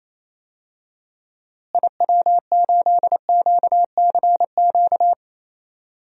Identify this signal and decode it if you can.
Morse code, heard as a keyed tone, about 28 wpm, 700 Hz. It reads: IW8QCQ
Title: DXCC-100 Call Signs - 6 Characters - Repeated 3x - 28wpm